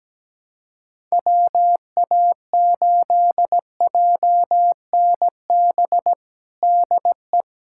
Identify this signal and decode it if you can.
Morse code, heard as a keyed tone, about 17 wpm, 700 Hz. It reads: WA8JNB DE
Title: USA Callsign DE Callsign - 6 Characters - 17wpm